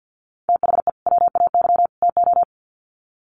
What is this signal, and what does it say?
Morse code, heard as a keyed tone, about 50 wpm, 700 Hz. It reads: THE WAY TO
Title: Sets of 3 Words 50wpm